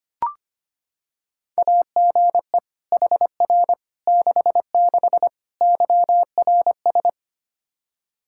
Morse code, heard as a keyed tone, about 25 wpm, 700 Hz. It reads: AGE HR 66 YRS
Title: Practice QSO - Exchange 3 - More Info - 25wpm